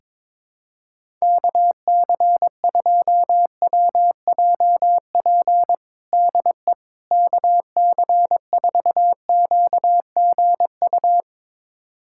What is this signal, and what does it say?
Morse code, heard as a keyed tone, about 21 wpm, 700 Hz. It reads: KC2WJP DE KC4QGU
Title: USA Callsign DE Callsign - 6 Characters - 22wpm